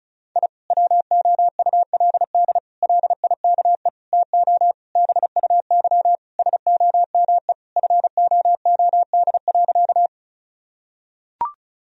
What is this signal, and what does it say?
Morse code, heard as a keyed tone, about 36 wpm, 700 Hz. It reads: I WOULD LIKE TO BUY SOME FOOD.
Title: Sentences from Top 1000 Words - Review Part 4 - 35wpm